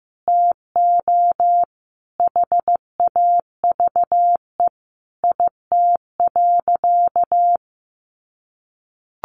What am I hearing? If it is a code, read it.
Morse code, heard as a keyed tone, about 15 wpm, 700 Hz. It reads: TO HAVE IT.
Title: Sentences from Top 100 Words 15wpm